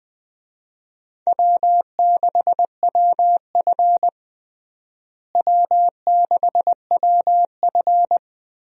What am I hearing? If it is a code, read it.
Morse code, heard as a keyed tone, about 21 wpm, 700 Hz. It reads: W6WF W6WF
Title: USA Call Signs - 4 Characters - Repeated 2x - 20wpm